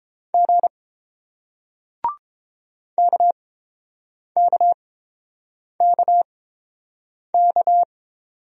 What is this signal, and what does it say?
Morse code, heard as a keyed tone, about 31 wpm, spaced as for 7 wpm, 700 Hz. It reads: G KKKK